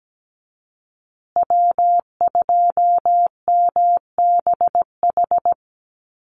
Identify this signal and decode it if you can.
Morse code, heard as a keyed tone, about 17 wpm, 700 Hz. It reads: W2MBH
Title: USA Call Signs Repeated 2x 17wpm